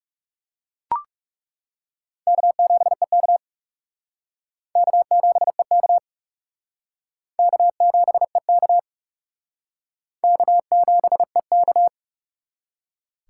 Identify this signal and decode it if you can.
Morse code, heard as a keyed tone, about 40 wpm, 700 Hz. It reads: K7EK K7EK K7EK K7EK